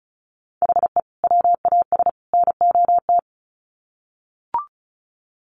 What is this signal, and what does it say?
Morse code, heard as a keyed tone, about 34 wpm, 700 Hz. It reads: HE WAS NOT
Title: Sets of 3 Words 35wpm